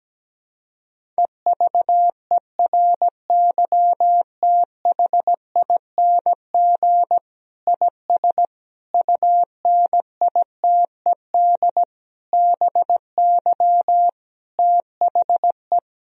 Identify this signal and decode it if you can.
Morse code, heard as a keyed tone, about 17 wpm, 700 Hz. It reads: EVERYTHING IS UNITED BY THE
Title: Sentences from Top 1000 Words - Review Part 4 - 17wpm